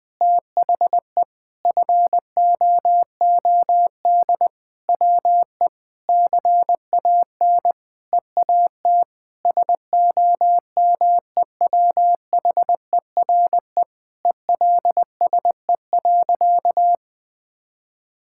Morse code, heard as a keyed tone, about 20 wpm, 700 Hz. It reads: THE FOOD WE CAN EAT SOMEWHERE ELSE.
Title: Sentences with 5-Word Phrases - 20wpm